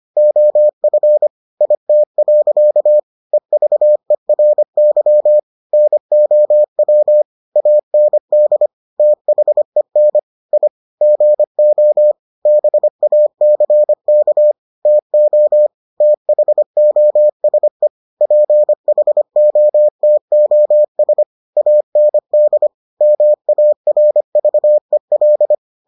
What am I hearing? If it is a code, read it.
Morse code, heard as a keyed tone, about 25 wpm, 600 Hz. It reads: OF IT. EVERY NOW AND THEN I GO BACK TO THOSE PHOTOS AND MARVEL